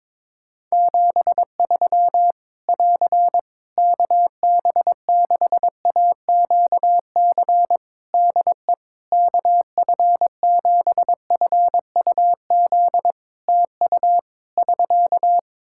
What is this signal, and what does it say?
Morse code, heard as a keyed tone, about 22 wpm, 700 Hz. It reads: 73 <AR> KB6AQC DE KF7FUZ TU <SK>
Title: Practice QSO - Exchange 4 - Ending - 22wpm